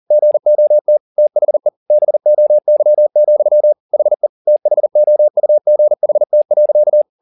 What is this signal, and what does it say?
Morse code, heard as a keyed tone, about 40 wpm, 600 Hz. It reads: GOT THE BOY, HE THOUGHT.